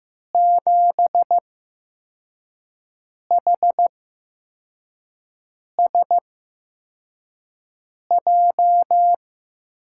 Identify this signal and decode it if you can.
Morse code, heard as a keyed tone, about 15 wpm, spaced as for 4 wpm, 700 Hz. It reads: 7HSJ
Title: USA Call Signs - 6 Characters - 8x Character Spacing - 15wpm